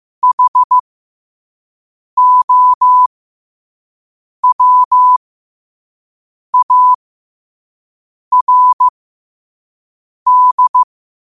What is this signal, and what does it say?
Morse code, heard as a keyed tone, about 15 wpm, spaced as for 5 wpm, 1000 Hz. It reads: HOWARD
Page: Quotes for Tue, 12 Aug 2025 in Morse Code at 5 words per minute.